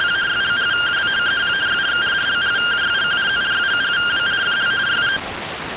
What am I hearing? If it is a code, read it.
Two-tone teleprinter signal, about 170 Hz shift.
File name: findings / axial iodine 10mm 184.gen-10 rtty rx